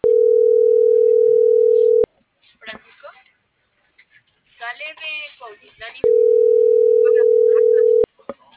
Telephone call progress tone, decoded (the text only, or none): ringback tone